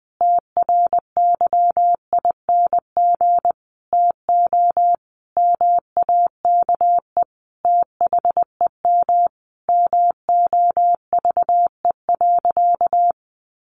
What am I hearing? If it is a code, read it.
Morse code, heard as a keyed tone, about 20 wpm, 700 Hz. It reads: TRYING TO MAKE THEM MOVE.
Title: A Sound Of Thunder 20wpm